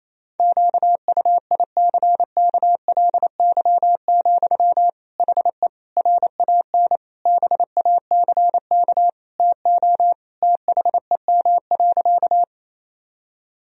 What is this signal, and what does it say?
Morse code, heard as a keyed tone, about 28 wpm, 700 Hz. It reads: QUICKLY, HE RAN BACK TO THEM.
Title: Sentences from Top 1000 Words - Review Part 3 - 28wpm